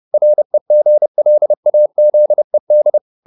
Morse code, heard as a keyed tone, about 30 wpm, 600 Hz. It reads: REGLAZED